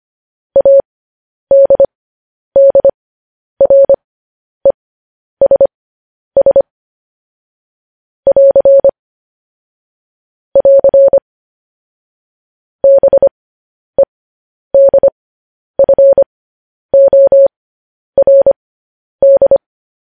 Morse code, heard as a keyed tone, about 25 wpm, spaced as for 10 wpm, 550 Hz. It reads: ADDRESS <AR> <AR> BEDFORD
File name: podcast / News headlines-1x-25x10